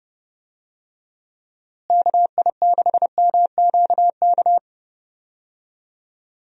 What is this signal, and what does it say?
Morse code, heard as a keyed tone, about 30 wpm, 700 Hz. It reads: KI6MQK
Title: USA Call Signs - 6 Characters - Repeated 3x - 30wpm